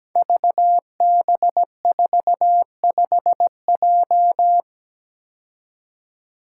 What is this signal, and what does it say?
Morse code, heard as a keyed tone, about 17 wpm, 700 Hz. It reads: VB45J
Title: Code Groups - 5 Letter:Numbers - 17wpm